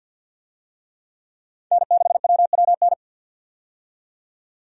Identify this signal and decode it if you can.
Morse code, heard as a keyed tone, about 47 wpm, 700 Hz. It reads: N6WWN
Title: USA Call Signs - 5 Characters - 50wpm